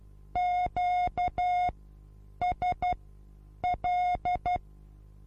Morse code, heard as a keyed tone, about 12 wpm, spaced as for 8 wpm, 720 Hz. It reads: QSL
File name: Onlinelehrgang / morsen / DJ4UF Morsen23